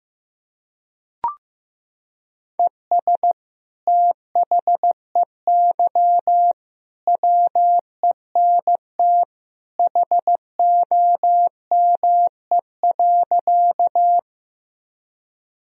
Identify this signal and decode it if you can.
Morse code, heard as a keyed tone, about 15 wpm, 700 Hz. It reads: ES THEY WENT HOME.
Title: Sentences from Top 200 Words and Abbreviations 15wpm